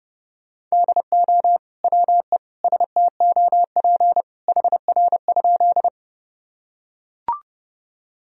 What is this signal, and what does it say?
Morse code, heard as a keyed tone, about 30 wpm, 700 Hz. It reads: DO WE STOP HR?
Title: Sentences from Top 300 Words and Abbreviations 30wpm